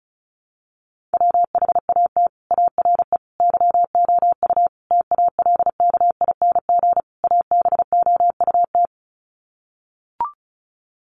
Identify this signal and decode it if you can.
Morse code, heard as a keyed tone, about 34 wpm, 700 Hz. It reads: WHAT ARE YOU TALKING ABOUT
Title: Sets of 5 Words 35wpm